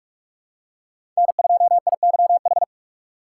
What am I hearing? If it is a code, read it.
Morse code, heard as a keyed tone, about 41 wpm, 700 Hz. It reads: N2IYH
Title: USA Call Signs - 3x Character Spacing - 45wpm